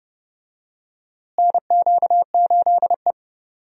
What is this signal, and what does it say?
Morse code, heard as a keyed tone, about 31 wpm, 700 Hz. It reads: NQ8E
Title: USA Call Signs - 4 Characters - Repeated 2x - 30wpm